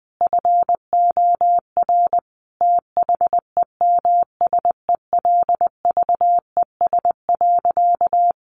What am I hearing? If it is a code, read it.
Morse code, heard as a keyed tone, about 20 wpm, 700 Hz. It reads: FOR THEMSELVES.
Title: Sentences from Top 1400 Words 20wpm